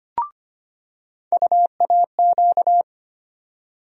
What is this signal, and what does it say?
Morse code, heard as a keyed tone, about 25 wpm, 700 Hz. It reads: UAQ
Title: USA Call Sign Postfixes - 3 Letters 25wpm